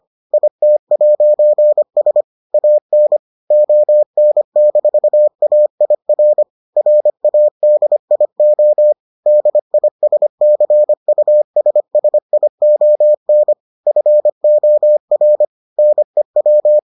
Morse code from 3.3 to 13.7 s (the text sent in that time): ON-AIR RADIO DISCUSSION